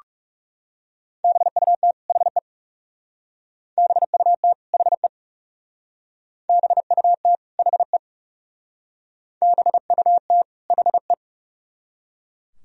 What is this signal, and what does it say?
Morse code, heard as a keyed tone, about 38 wpm, 700 Hz. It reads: BUT HE BUT HE BUT HE BUT HE